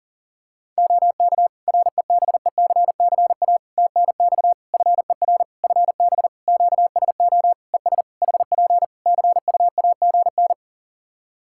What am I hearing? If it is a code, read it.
Morse code, heard as a keyed tone, about 40 wpm, 700 Hz. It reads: OK REBECCA TNX FER FB QSO ES HP CUAGN
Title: Practice QSO - Exchange 4 - Ending - 40wpm